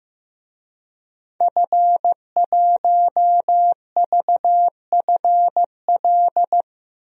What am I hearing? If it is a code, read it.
Morse code, heard as a keyed tone, about 15 wpm, 700 Hz. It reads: F1VFL
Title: DXCC-100 Call Signs - 5 Characters - Repeated 3x - 15wpm